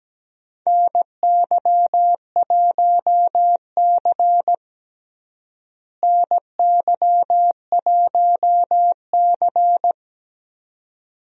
Morse code, heard as a keyed tone, about 17 wpm, 700 Hz. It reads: NY1C NY1C
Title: USA Call Signs - 4 Characters - Repeated 2x - 17wpm